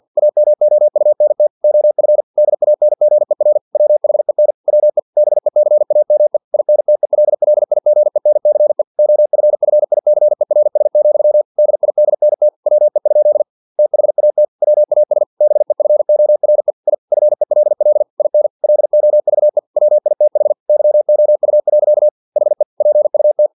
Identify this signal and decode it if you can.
Morse code, heard as a keyed tone, about 50 wpm, 600 Hz. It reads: AMOUNT OF DANGER WHEN WE BECAME INTELLIGENCE OFFICERS, DIDNT WE? THAT WAS BEFORE I FELL IN LOVE WITH YOU. HE PUT
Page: Never 49wpm